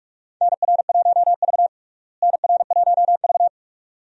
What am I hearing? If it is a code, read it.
Morse code, heard as a keyed tone, about 46 wpm, 700 Hz. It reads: NR1V NR1V
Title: USA Call Signs - 4 Characters - Repeated 2x - 45wpm